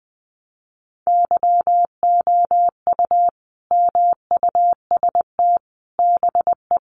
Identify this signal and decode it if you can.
Morse code, heard as a keyed tone, about 20 wpm, 700 Hz. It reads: YOU MUST BE
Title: Sentences from Top 1200 Words 20wpm